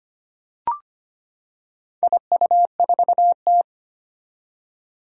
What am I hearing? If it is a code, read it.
Morse code, heard as a keyed tone, about 25 wpm, 700 Hz. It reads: IU4T
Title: Four Challenging Letter-Number Combinations 25wpm